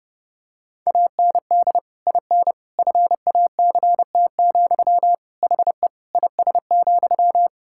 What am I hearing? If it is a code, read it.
Morse code, heard as a keyed tone, about 32 wpm, 700 Hz. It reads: AND IN FACT, HE IS,
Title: Sentences from Top 1600 Words 30wpm